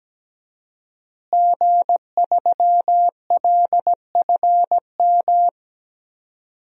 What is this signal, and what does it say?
Morse code, heard as a keyed tone, about 17 wpm, 700 Hz. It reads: G3LFM
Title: DXCC-100 Call Signs - 5 Characters - Repeated 3x - 17wpm